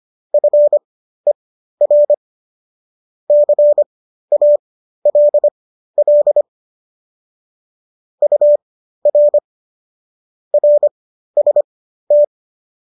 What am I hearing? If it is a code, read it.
Morse code, heard as a keyed tone, about 25 wpm, spaced as for 13 wpm, 590 Hz. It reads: FER CALL UR RST